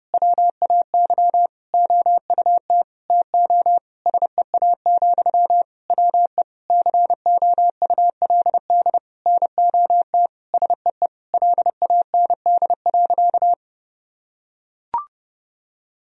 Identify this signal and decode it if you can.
Morse code, heard as a keyed tone, about 32 wpm, 700 Hz. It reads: WAY OUT TO SEA, WE COULD NOT SEE LAND.
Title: Sentences from Top 1000 Words - Review Part 2 - 30wpm